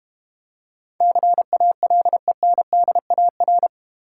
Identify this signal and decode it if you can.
Morse code, heard as a keyed tone, about 30 wpm, 700 Hz. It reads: CALENDAR
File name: Easy Words - Speed-Racing with Hints - 25wpm